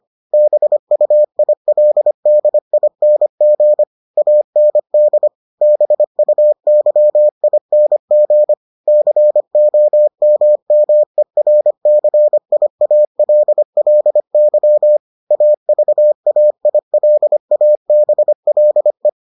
Morse code, heard as a keyed tone, about 25 wpm, 600 Hz. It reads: BUILDING AND BUYING COMMERCIALLY AVAILABLE